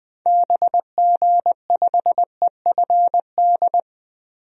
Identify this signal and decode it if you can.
Morse code, heard as a keyed tone, about 20 wpm, 700 Hz. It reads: BG5EFD
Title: DXCC-100 Call Signs - 6 Characters - 20wpm